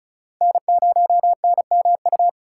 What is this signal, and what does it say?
Morse code, heard as a keyed tone, about 36 wpm, 700 Hz. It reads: N0NMU
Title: USA Call Signs - 5 Characters - 4x Character Spacing - 35wpm